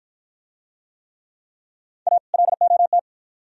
Morse code, heard as a keyed tone, about 52 wpm, 700 Hz. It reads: A LOT